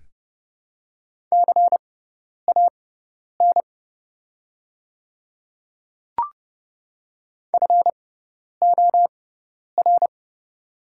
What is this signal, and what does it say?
Morse code, heard as a keyed tone, about 30 wpm, spaced as for 10 wpm, 700 Hz. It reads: CAN FOR
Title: 6x Character Spacing - Top 200 Words - 30wpm